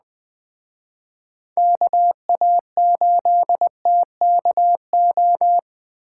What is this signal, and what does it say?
Morse code, heard as a keyed tone, about 21 wpm, 700 Hz. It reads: KA8TKO
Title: Speed Racing - USA 2x3 Call Signs 17wpm